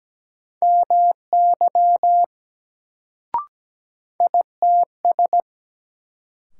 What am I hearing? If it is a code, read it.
Morse code, heard as a keyed tone, about 17 wpm, 700 Hz. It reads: MY ITS
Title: Sets of 2 Words - New Word Drill - 17wpm